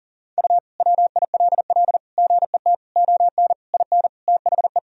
Morse code, heard as a keyed tone, about 40 wpm, 700 Hz. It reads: U WILL GET ON IN THE
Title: Sentences from Top 200 Words and Abbreviations 40wpm